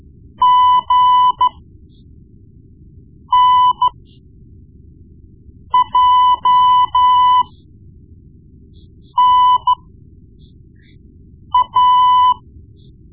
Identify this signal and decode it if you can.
Morse code, heard as a keyed tone, about 13 wpm, spaced as for 4 wpm, 1000 Hz. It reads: GNJNA
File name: Morse Lessons / Morse Lessons / Tape-1B